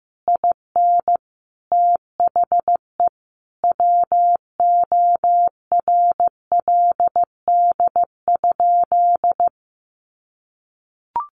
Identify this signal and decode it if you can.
Morse code, heard as a keyed tone, about 15 wpm, 700 Hz. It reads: IN THE WORLD?
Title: Sentences from Top 1400 Words 15wpm